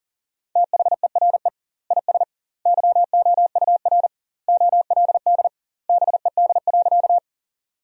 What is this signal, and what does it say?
Morse code, heard as a keyed tone, about 37 wpm, 700 Hz. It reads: THERE IS YOUR OLD BED.
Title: 2x Word-Spacing - Sentences from Top 500 Words - 40wpm